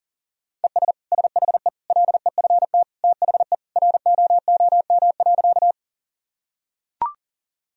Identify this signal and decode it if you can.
Morse code, heard as a keyed tone, about 37 wpm, 700 Hz. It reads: ES SHE LEFT THE ROOM.
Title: Sentences from Top 300 Words and Abbreviations 40wpm